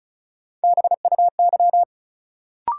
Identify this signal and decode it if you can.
Morse code, heard as a keyed tone, about 34 wpm, 700 Hz. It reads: BUY
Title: Sentences from Top 700 Words - New Word Drill - 35wpm